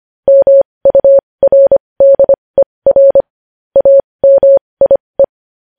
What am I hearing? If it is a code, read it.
Morse code, heard as a keyed tone, about 25 wpm, spaced as for 20 wpm, 550 Hz. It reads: MURDER AMIE
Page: news-headlines-in-morse-code-at-20-wpm